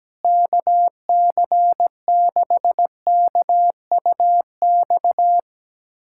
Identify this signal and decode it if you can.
Morse code, heard as a keyed tone, about 17 wpm, 700 Hz. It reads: KC6KUX
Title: USA Callsign DE Callsign - 6 Characters - 17wpm